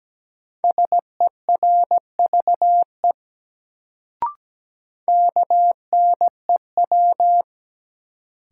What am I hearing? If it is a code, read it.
Morse code, heard as a keyed tone, about 17 wpm, 700 Hz. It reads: SERVE KNEW
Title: Sentences from Top 400 Words - New Word Drill - 17wpm